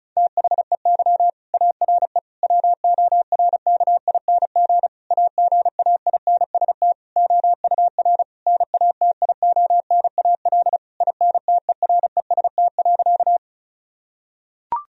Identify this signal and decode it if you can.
Morse code, heard as a keyed tone, about 36 wpm, 700 Hz. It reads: THEY ARE WORKING AGAINST OUR NATIONAL INTEREST.
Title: Sentences from Top 1000 Words - Review Part 4 - 35wpm